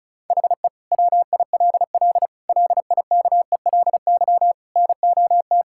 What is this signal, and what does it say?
Morse code, heard as a keyed tone, about 36 wpm, 700 Hz. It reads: HE WILL LIKELY NOT
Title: Sentences from Top 1000 Words - Review Part 5 - 35wpm